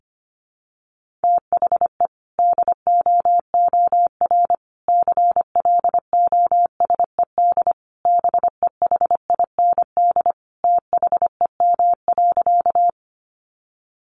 Morse code, heard as a keyed tone, about 25 wpm, 700 Hz. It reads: THE DOOR CLOSED BEHIND THEM.
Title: Sentences from Top 1400 Words 25wpm